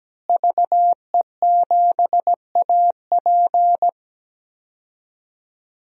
Morse code, heard as a keyed tone, about 17 wpm, 700 Hz. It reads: VE7AP
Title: DXCC-100 Call Signs - 5 Characters - Repeated 3x - 17wpm